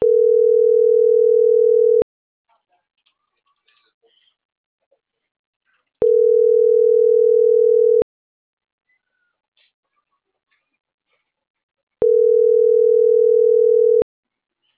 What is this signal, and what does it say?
Ringback tone (North America), 440 + 480 Hz.